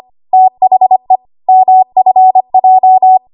Morse code, heard as a keyed tone, about 25 wpm, 750 Hz. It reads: THE MFJ
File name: Morse / 25WPM